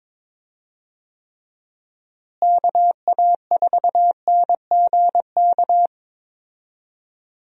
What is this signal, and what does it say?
Morse code, heard as a keyed tone, about 22 wpm, 700 Hz. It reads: KA4NGK
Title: USA Call Signs - 6 Characters - Repeated 3x - 22wpm